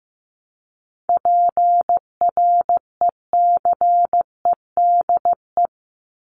Morse code, heard as a keyed tone, about 15 wpm, 700 Hz. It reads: PRECEDE
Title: ING Suffix - Top 100 Words with Base Words 15wpm